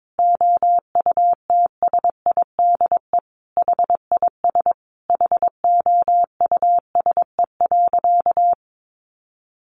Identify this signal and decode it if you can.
Morse code, heard as a keyed tone, about 22 wpm, 700 Hz. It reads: OUTSIDE HIS HOUSE.
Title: Additional Easy Phrases 22wpm